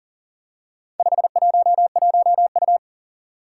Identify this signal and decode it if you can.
Morse code, heard as a keyed tone, about 38 wpm, 700 Hz. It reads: 511U
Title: Four Challenging Letter-Number Combinations 40wpm